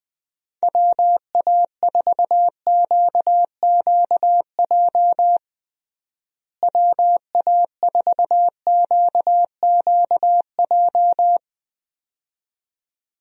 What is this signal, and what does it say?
Morse code, heard as a keyed tone, about 20 wpm, 700 Hz. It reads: WA4QQJ WA4QQJ
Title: USA Call Signs - 6 Characters - Repeated 2x - 20wpm